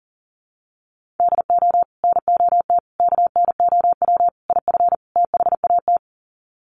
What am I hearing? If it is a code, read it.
Morse code, heard as a keyed tone, about 40 wpm, 700 Hz. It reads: DO NOT KNOW IF THAT
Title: Sets of 5 Words Encore2 40wpm